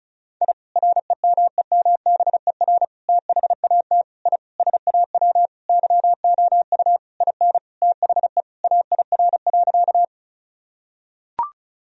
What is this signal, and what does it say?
Morse code, heard as a keyed tone, about 36 wpm, 700 Hz. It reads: I REMEMBER THAT I SAW YOU IN THE AIR.
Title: Sentences from Top 1000 Words - Review Part 1 - 35wpm